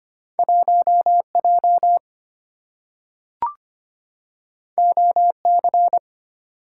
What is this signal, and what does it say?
Morse code, heard as a keyed tone, about 25 wpm, 690 Hz. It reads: 1J OC
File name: Two Letter-Number Combinations - 25wpm